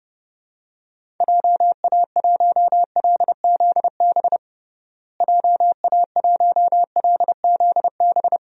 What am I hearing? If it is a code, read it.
Morse code, heard as a keyed tone, about 32 wpm, 700 Hz. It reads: JA1LZB JA1LZB
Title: DXCC-100 Call Signs - 6 Characters - Repeated 2x - 30wpm